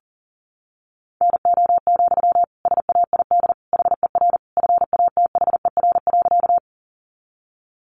Morse code, heard as a keyed tone, about 40 wpm, 700 Hz. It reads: NO, SAID HER FATHER.
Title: Sentences from Top 300 Words 40wpm